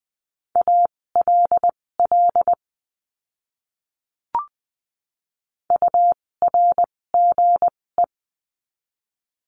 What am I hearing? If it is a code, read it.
Morse code, heard as a keyed tone, about 20 wpm, spaced as for 16 wpm, 700 Hz. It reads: ALL URGE
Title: 256 Lesson 42a - Words 12-20wpm